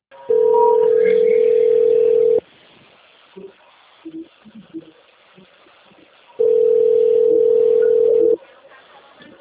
Ringback tone (North America), 440 + 480 Hz.